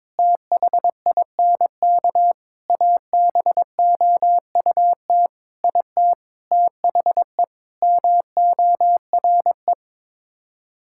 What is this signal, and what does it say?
Morse code, heard as a keyed tone, about 22 wpm, 700 Hz. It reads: THINK ABOUT IT THE MORE
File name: practice / 22wpm / Sets of 8 Words - 22wpm